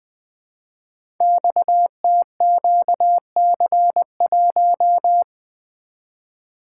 Morse code, heard as a keyed tone, about 20 wpm, 700 Hz. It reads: XTQC1
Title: Code Groups - 5 Letter:Numbers - 20wpm